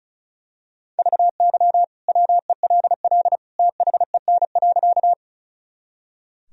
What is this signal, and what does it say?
Morse code, heard as a keyed tone, about 36 wpm, 700 Hz. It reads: VY WELL THEN.
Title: Sentences from Top 200 Words and Abbreviations 35wpm